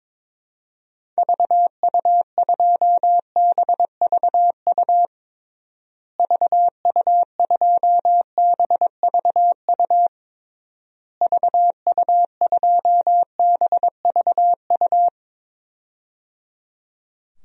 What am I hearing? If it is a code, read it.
Morse code, heard as a keyed tone, about 22 wpm, 700 Hz. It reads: VU2BVU VU2BVU VU2BVU
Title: DXCC-100 Call Signs - 6 Characters - Repeated 3x - 22wpm